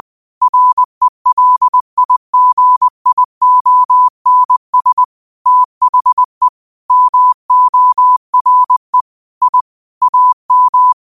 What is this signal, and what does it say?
Morse code, heard as a keyed tone, about 20 wpm, 1000 Hz. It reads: RELIGIONS THE MORE I AM
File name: qotdfiles / 20wpm